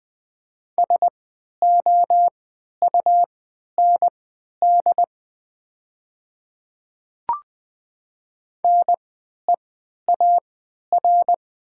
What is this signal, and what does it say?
Morse code, heard as a keyed tone, about 20 wpm, spaced as for 11 wpm, 690 Hz. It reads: SOUND NEAR
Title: 3x Character Spacing - Top 200 Words - 20wpm